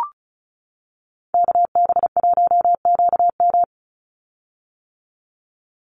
Morse code, heard as a keyed tone, about 34 wpm, 700 Hz. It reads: KB1QM
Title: USA Call Signs 35wpm